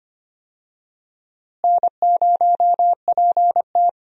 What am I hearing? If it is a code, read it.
Morse code, heard as a keyed tone, about 25 wpm, 700 Hz. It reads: N0PT
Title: USA Call Signs - 4 Characters - Repeated 3x - 25wpm